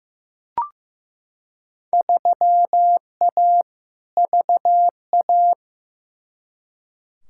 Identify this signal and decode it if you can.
Morse code, heard as a keyed tone, about 15 wpm, 700 Hz. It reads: 3A VA
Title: Field Day Exchanges - 15wpm